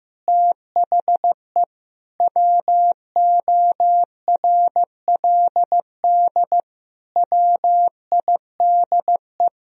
Morse code, heard as a keyed tone, about 15 wpm, 700 Hz. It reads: THE WORLD WIDE
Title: Sentences from Top 1000 Words - Review Part 4 - 15wpm